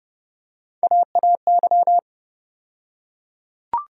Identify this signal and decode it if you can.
Morse code, heard as a keyed tone, about 30 wpm, 700 Hz. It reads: AAY
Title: USA Call Sign Postfixes - 3 Letters 30wpm